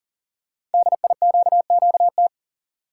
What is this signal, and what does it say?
Morse code, heard as a keyed tone, about 42 wpm, 690 Hz. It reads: DIQQT
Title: Code Groups - 5 Letters - 40wpm